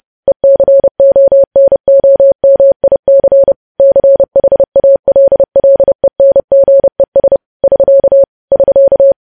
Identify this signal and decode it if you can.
Morse code, heard as a keyed tone, about 30 wpm, 550 Hz. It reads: ECONOMIC CHALLENGES <SK> <SK>
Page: news headlines in morse code at 30 wpm